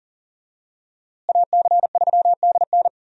Morse code, heard as a keyed tone, about 40 wpm, 700 Hz. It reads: AC3DN
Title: USA Call Signs - 5 Characters - Repeated 3x - 40wpm